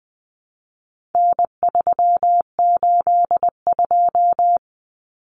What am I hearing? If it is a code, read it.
Morse code, heard as a keyed tone, about 20 wpm, 700 Hz. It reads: N382
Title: Common Cut Numbers - 4-Digit 20wpm